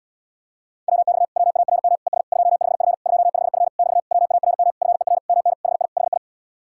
Morse code, heard as a keyed tone, about 77 wpm, 700 Hz. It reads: WX WARM ES 25F 25F <BT> WARM FER MN HI HI